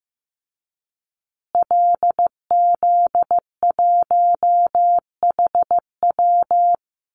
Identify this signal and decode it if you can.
Morse code, heard as a keyed tone, about 15 wpm, 700 Hz. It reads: LZ1HW
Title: Call Signs Repeated 3x - 15wpm